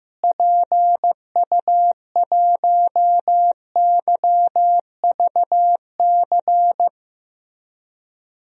Morse code, heard as a keyed tone, about 15 wpm, 700 Hz. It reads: PU1YVC
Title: DXCC-100 Call Signs - 6 Characters - Repeated 3x - 15wpm